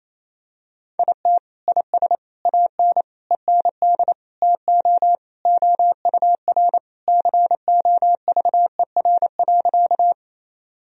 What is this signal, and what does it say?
Morse code, heard as a keyed tone, about 27 wpm, 700 Hz. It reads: IT IS AN END TO OUR COVER.
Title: Sentences from Top 1000 Words - Review Part 1 - 28wpm